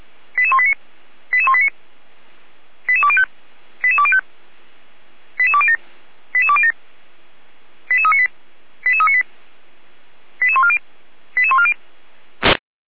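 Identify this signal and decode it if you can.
Five-tone selective call (ZVEI), heard as quick runs of five tones, tones of about 70 ms. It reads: 88189 88189 88205 88205 88207 88207 88398 88398 88140 88140
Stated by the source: ZVEI five-tone selective call received on SDR.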